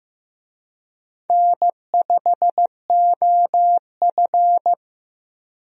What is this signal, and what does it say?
Morse code, heard as a keyed tone, about 15 wpm, 700 Hz. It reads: N5OF
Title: USA Call Signs - 4 Characters - Repeated 3x - 15wpm